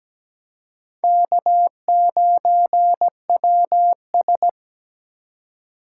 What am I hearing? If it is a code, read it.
Morse code, heard as a keyed tone, about 17 wpm, 700 Hz. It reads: K9WS
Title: USA Call Signs - 3x Character Spacing - 17wpm